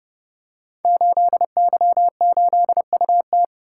Morse code, heard as a keyed tone, about 29 wpm, 700 Hz. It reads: 8Y8UT
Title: Code Groups - 5 Letter:Numbers - 30wpm